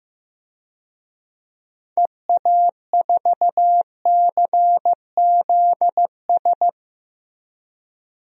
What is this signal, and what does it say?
Morse code, heard as a keyed tone, about 16 wpm, 700 Hz. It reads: EA4CZS
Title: DXCC-100 Call Signs - 6 Characters - Repeated 3x - 15wpm